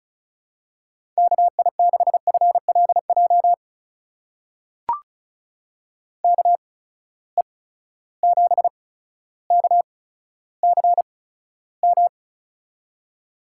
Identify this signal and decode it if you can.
Morse code, heard as a keyed tone, about 36 wpm, spaced as for 15 wpm, 700 Hz. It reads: KI6FLJ KE7KCM
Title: USA Call Signs - 6 Characters - 8x Character Spacing - 35wpm